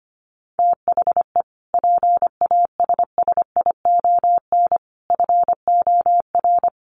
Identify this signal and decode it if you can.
Morse code, heard as a keyed tone, about 25 wpm, 700 Hz. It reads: THE PASSION FOR
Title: Sentences from Top 1300 Words 25wpm